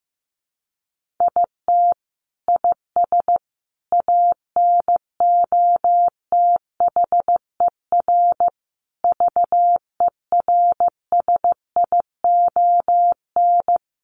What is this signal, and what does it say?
Morse code, heard as a keyed tone, about 15 wpm, 700 Hz. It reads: IT IS ANOTHER VERSION
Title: Sentences from Top 1400 Words 15wpm